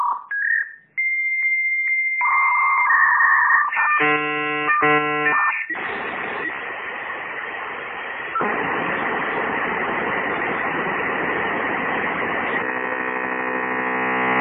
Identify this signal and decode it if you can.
Dial-up modem handshake.